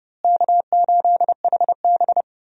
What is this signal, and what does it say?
Morse code, heard as a keyed tone, about 31 wpm, 700 Hz. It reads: K8HB
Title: USA Call Signs - 4 Characters - Repeated 2x - 30wpm